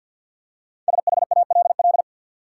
Morse code, heard as a keyed tone, about 53 wpm, 700 Hz. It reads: SHALL
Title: Five Letter Words - 50wpm